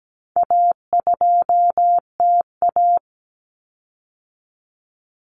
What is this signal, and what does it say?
Morse code, heard as a keyed tone, about 17 wpm, 700 Hz. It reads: A2TA
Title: Common Cut Numbers - 4-Digit 17wpm